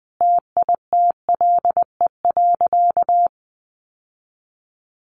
Morse code, heard as a keyed tone, about 20 wpm, 700 Hz. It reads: TITLE.
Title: Sentences from Top 1200 Words 20wpm